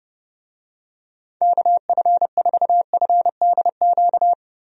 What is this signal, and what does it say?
Morse code, heard as a keyed tone, about 31 wpm, 700 Hz. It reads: KF4FDQ
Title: USA Call Signs - 6 Characters - Repeated 3x - 30wpm